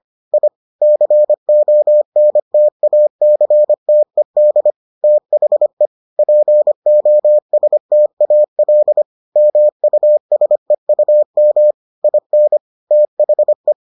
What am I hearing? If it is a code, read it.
Morse code, heard as a keyed tone, about 25 wpm, 600 Hz. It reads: I CONTACTED THE POSTAL MUSEUM IN THE